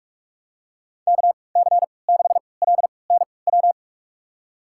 Morse code, heard as a keyed tone, about 46 wpm, spaced as for 26 wpm, 700 Hz. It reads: KC6LNW
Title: USA Call Signs - 6 Characters - 3x Character Spacing - 45wpm